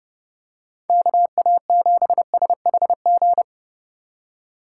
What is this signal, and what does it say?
Morse code, heard as a keyed tone, about 30 wpm, 700 Hz. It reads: KA7SHG